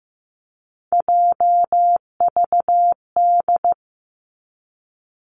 Morse code, heard as a keyed tone, about 15 wpm, 700 Hz. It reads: JVD
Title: Three Challenging Letter-Number Combinations 15wpm